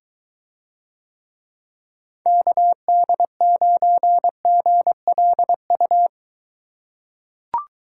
Morse code, heard as a keyed tone, about 23 wpm, 700 Hz. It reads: KD9GLU